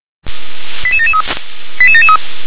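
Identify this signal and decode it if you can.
Five-tone selective call (ZVEI), heard as quick runs of five tones, tones of about 70 ms. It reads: 88773 88773
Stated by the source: ZVEI five-tone selective call received by SDR.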